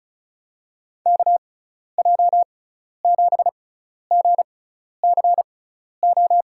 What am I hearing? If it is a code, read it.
Morse code, heard as a keyed tone, about 36 wpm, spaced as for 12 wpm, 700 Hz. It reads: KJ7GCO
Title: USA Call Signs - 6x Character Spacing - 35wpm